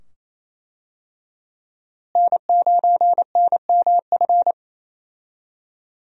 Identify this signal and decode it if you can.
Morse code, heard as a keyed tone, about 27 wpm, 700 Hz. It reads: N9NMF
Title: USA Call Signs - 5 Characters - Repeated 2x - 28wpm